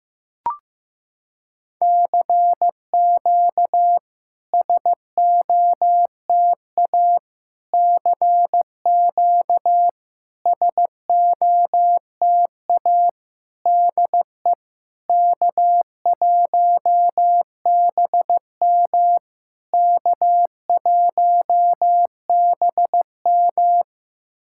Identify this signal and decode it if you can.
Morse code, heard as a keyed tone, about 15 wpm, 700 Hz. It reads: CQ SOTA CQ SOTA DE K1BM K1BM
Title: SOTA QSO Practice 15wpm